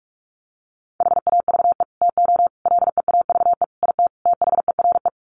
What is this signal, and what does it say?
Morse code, heard as a keyed tone, about 44 wpm, 700 Hz. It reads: HAVE TO LEAVE IT THERE
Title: Sets of 5 Words Encore2 45wpm